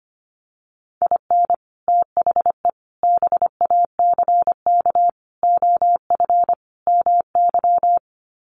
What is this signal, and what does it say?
Morse code, heard as a keyed tone, about 25 wpm, 700 Hz. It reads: IN THE BACK OF MY
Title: Sets of 5 Words Encore 25wpm